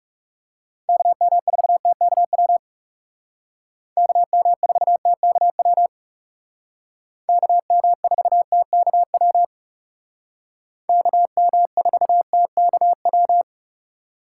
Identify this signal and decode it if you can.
Morse code, heard as a keyed tone, about 38 wpm, 700 Hz. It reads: KM4TKW KM4TKW KM4TKW KM4TKW